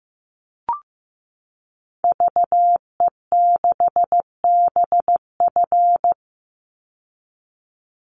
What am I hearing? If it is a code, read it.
Morse code, heard as a keyed tone, about 15 wpm, 700 Hz. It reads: VE6BF
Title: Call Signs Repeated 2x - 15wpm